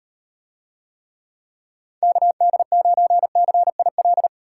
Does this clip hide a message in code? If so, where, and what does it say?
Yes: Morse code from 2.0 to 4.3 s, heard as a keyed tone, about 38 wpm, 700 Hz. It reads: KD9CIL